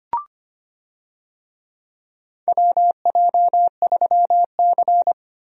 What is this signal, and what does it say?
Morse code, heard as a keyed tone, about 25 wpm, 700 Hz. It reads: WJ3C
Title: USA Call Signs - 4 Characters - Repeated 2x - 25wpm